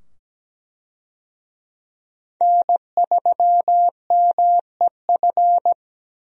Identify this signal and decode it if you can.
Morse code, heard as a keyed tone, about 17 wpm, 700 Hz. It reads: N3MEF
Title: USA Call Signs - 5 Characters - 17wpm